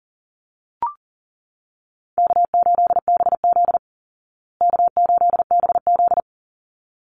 Morse code, heard as a keyed tone, about 40 wpm, 700 Hz. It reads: K8BZ K8BZ
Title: Easy Call Signs Repeated 2x - 40wpm